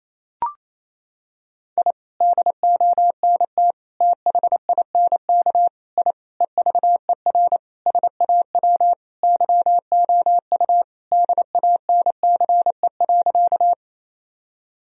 Morse code, heard as a keyed tone, about 28 wpm, 700 Hz. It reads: I DONT THINK I EVER SAW YOU DANCE.
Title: Sentences from Top 1000 Words - Review Part 4 - 28wpm